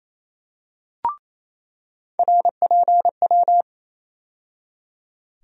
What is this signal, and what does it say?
Morse code, heard as a keyed tone, about 28 wpm, 700 Hz. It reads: RPW
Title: USA Call Sign Postfixes - 3 Letters 28wpm